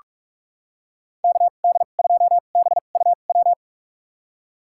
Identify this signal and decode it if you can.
Morse code, heard as a keyed tone, about 46 wpm, spaced as for 32 wpm, 700 Hz. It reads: KD2BUW
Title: USA Call Signs - 2x Character Spacing - 45wpm